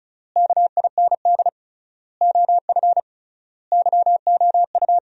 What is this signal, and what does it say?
Morse code, heard as a keyed tone, about 36 wpm, 700 Hz. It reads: KIND OF YOU
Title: 3x Word-Spacing - Sentences from Top 300 Words - 35wpm